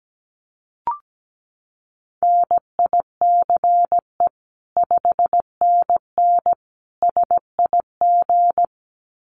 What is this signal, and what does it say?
Morse code, heard as a keyed tone, about 17 wpm, 700 Hz. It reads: NICE 5NN SIG
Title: Common QSO Phrases 17wpm